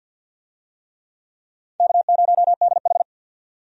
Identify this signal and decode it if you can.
Morse code, heard as a keyed tone, about 52 wpm, 700 Hz. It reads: K0DH
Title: USA Call Signs - 4 Characters - 50wpm